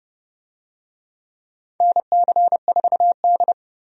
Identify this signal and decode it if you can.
Morse code, heard as a keyed tone, about 30 wpm, 700 Hz. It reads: NC4D
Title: USA Call Signs - 4 Characters - 30wpm